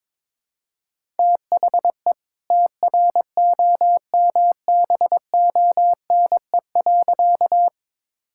Morse code, heard as a keyed tone, about 22 wpm, 690 Hz. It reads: THE TROMBONE.
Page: morse-code-ninja-practice-22wpm